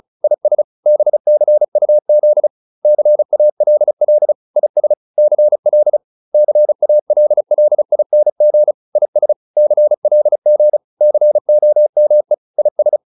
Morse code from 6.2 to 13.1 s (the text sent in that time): CALLING IS CLG COME IS